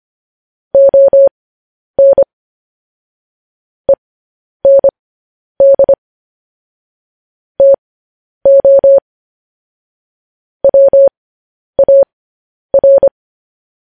Morse code, heard as a keyed tone, about 25 wpm, spaced as for 10 wpm, 550 Hz. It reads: ON END TO WAR
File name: podcast / News headlines-1x-25x10